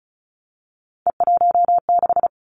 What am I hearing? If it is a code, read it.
Morse code, heard as a keyed tone, about 36 wpm, 700 Hz. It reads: E16
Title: Three Challenging Letter-Number Combinations 35wpm